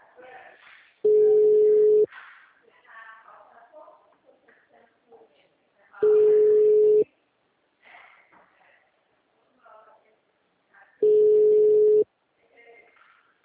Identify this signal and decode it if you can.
Ringback tone (Europe, 425 Hz), with speech.
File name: out- -2010- - - .13022